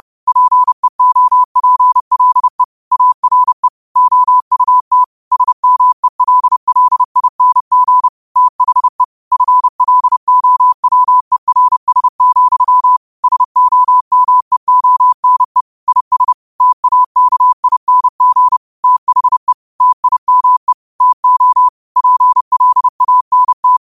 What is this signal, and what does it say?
Morse code, heard as a keyed tone, about 30 wpm, 1000 Hz. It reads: PEOPLE ARE OUT SMELLING THE FLOWERS, SOMEONE IS TAKING THE TIME TO PLANT